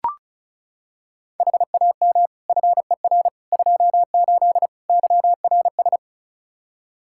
Morse code, heard as a keyed tone, about 36 wpm, 700 Hz. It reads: HAM FER 28 YRS
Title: Practice QSO - Exchange 3 - More Info - 35wpm